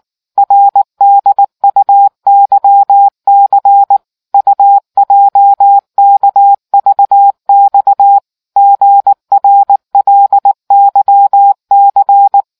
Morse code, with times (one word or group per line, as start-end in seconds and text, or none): 0.4-4.0 RDUYC
4.3-8.2 UJKVX
8.6-12.4 GRLYC